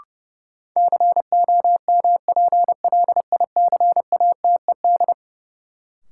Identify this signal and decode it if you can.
Morse code, heard as a keyed tone, about 32 wpm, 700 Hz. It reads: COMPLICATED
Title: Sentences from Top 1200 Words - New Word Drill - 30wpm